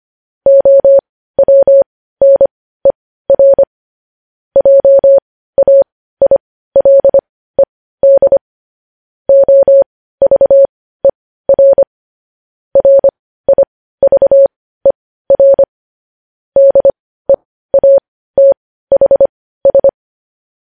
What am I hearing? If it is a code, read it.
Morse code, heard as a keyed tone, about 25 wpm, spaced as for 15 wpm, 550 Hz. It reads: OWNER JAILED OVER RIVER DEATHS
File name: podcast / News headlines-1x-25x15